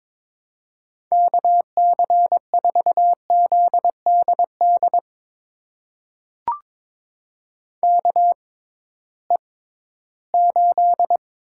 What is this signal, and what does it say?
Morse code, heard as a keyed tone, about 23 wpm, spaced as for 14 wpm, 700 Hz. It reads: KC4ZDD KE8
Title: USA Call Signs - 6 Characters - 6x Character Spacing - 22wpm